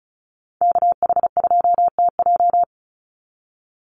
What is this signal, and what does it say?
Morse code, heard as a keyed tone, about 34 wpm, 700 Hz. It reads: KH2TJ
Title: NA SOTA Chaser Callsigns 35wpm